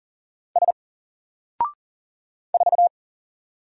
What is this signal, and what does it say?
Morse code, heard as a keyed tone, about 40 wpm, 700 Hz. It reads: S 4
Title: Single Letter-Number Alt 40wpm